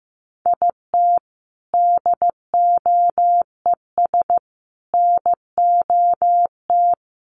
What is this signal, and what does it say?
Morse code, heard as a keyed tone, about 15 wpm, 700 Hz. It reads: IT DOES NOT
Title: Sets of 5 Words Encore2 15wpm